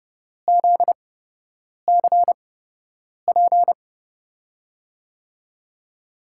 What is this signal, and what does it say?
Morse code, heard as a keyed tone, about 30 wpm, spaced as for 8 wpm, 700 Hz. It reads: ZCP
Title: USA Call Signs - 8x Character Spacing - 30wpm